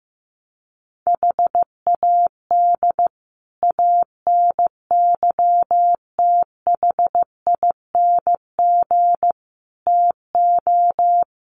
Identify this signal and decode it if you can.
Morse code, heard as a keyed tone, about 15 wpm, 700 Hz. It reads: HAD ANYTHING TO
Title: Sets of 5 Words Encore 15wpm